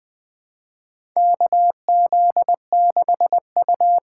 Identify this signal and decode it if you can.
Morse code, heard as a keyed tone, about 21 wpm, 700 Hz. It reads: KZ6U
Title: USA Call Signs - 4x Character Spacing - 20wpm